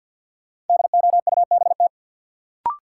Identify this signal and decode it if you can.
Morse code, heard as a keyed tone, about 53 wpm, 700 Hz. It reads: DOUBT
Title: Sets of 5 Words - New Word Drill - 50wpm